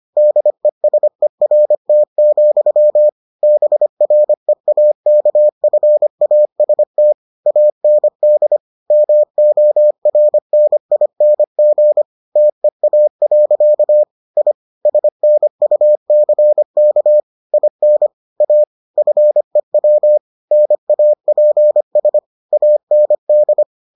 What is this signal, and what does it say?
Morse code, heard as a keyed tone, about 25 wpm, 600 Hz. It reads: DESERT, BREAKFAST AND MORNING TEA. I SNUCK IN A FEW NAPS AND